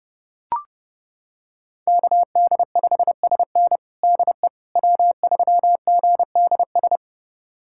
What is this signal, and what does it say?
Morse code, heard as a keyed tone, about 30 wpm, 700 Hz. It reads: KD5SN DE W3GDS
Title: USA Callsign DE Callsign 30wpm